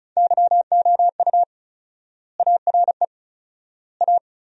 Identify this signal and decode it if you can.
Morse code, heard as a keyed tone, about 36 wpm, 700 Hz. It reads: YOU ARE A
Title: 4x Word-Spacing - Sentences from Top 400 Words - 35wpm